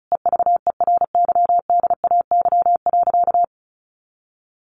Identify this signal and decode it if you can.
Morse code, heard as a keyed tone, about 34 wpm, 700 Hz. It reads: EVERYDAY.
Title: Additional Easy Phrases 35wpm